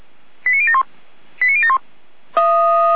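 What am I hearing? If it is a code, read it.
Five-tone selective call (ZVEI), heard as quick runs of five tones, tones of about 70 ms. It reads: 89071 89071
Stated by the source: ZVEI five-tone selective call 89110, received by SDR.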